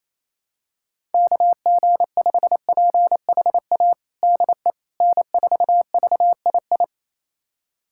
Morse code, heard as a keyed tone, about 28 wpm, 700 Hz. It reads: KG5PHA DE N4VII
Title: USA Callsign DE Callsign 28wpm